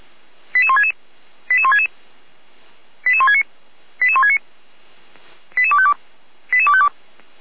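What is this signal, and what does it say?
Five-tone selective call (ZVEI), heard as quick runs of five tones, tones of about 70 ms. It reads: 88166 88166 88169 88169 88252 88252
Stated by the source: ZVEI five-tone call 88253 received on SDR, logged by fire brigade Feuerwehr Orlamünde.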